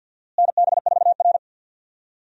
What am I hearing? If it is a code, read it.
Morse code, heard as a keyed tone, about 52 wpm, 700 Hz. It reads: NB4R
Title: USA Call Signs - 4 Characters - Repeated 2x - 50wpm